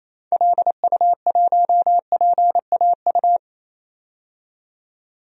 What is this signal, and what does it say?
Morse code, heard as a keyed tone, about 27 wpm, 700 Hz. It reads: LU1PAU
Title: DXCC-100 Call Signs - 6 Characters - 28wpm